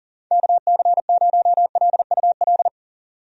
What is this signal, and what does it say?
Morse code, heard as a keyed tone, about 40 wpm, 700 Hz. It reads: KC0LUL
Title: USA Call Signs - 6 Characters - 40wpm